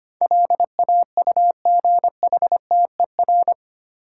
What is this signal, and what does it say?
Morse code, heard as a keyed tone, about 25 wpm, 700 Hz. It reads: LAUGHTER
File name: Eight Letter Words - 25wpm